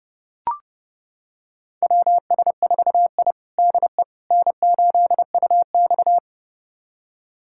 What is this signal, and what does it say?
Morse code, heard as a keyed tone, about 31 wpm, 700 Hz. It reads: WS4I DE N8UX
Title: USA Callsign DE Callsign - 4 Characters - 30wpm